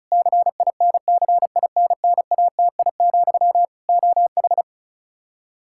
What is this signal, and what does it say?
Morse code, heard as a keyed tone, about 36 wpm, 700 Hz. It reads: CINCINNATI, OH
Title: Practice QSO - Exchange 1 - Beginning - 35wpm